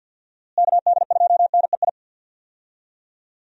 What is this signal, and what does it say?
Morse code, heard as a keyed tone, about 53 wpm, 700 Hz. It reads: KD2NEI
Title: USA Call Signs - 6 Characters - Repeated 2x - 50wpm